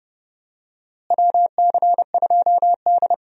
Morse code, heard as a keyed tone, about 32 wpm, 700 Hz. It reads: WC2D
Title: USA Call Signs - 4 Characters - Repeated 3x - 30wpm